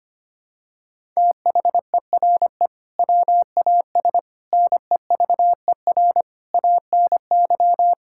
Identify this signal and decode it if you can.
Morse code, heard as a keyed tone, about 25 wpm, 700 Hz. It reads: THERE WAS NEVER ANY
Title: Sentences from Top 1000 Words - Review Part 6 - 25wpm